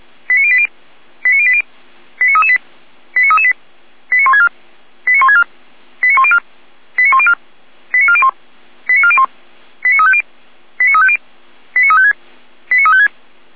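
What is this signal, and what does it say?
Five-tone selective call (ZVEI), heard as quick runs of five tones, tones of about 70 ms. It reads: 89080 89080 89338 89338 89164 89164 89104 89104 89401 89401 89350 89350 89356 89356
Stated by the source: ZVEI five-tone selective call 89235 received on SDR.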